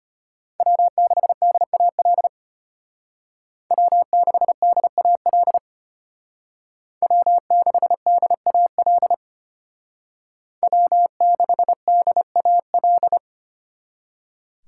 Morse code, heard as a keyed tone, about 33 wpm, 700 Hz. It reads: W6DAL W6DAL W6DAL W6DAL